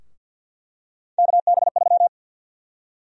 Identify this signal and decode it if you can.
Morse code, heard as a keyed tone, about 49 wpm, 700 Hz. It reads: KB3
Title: USA Call Sign Prefixes - Repeated 2x - 50wpm